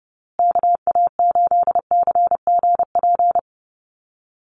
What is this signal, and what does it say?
Morse code, heard as a keyed tone, about 30 wpm, 700 Hz. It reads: KA8CGP
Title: USA Call Signs Repeated 2x 30wpm